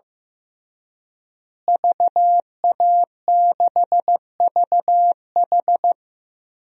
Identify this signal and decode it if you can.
Morse code, heard as a keyed tone, about 16 wpm, 700 Hz. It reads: VA6VH
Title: DXCC-100 Call Signs - 5 Characters - Repeated 3x - 15wpm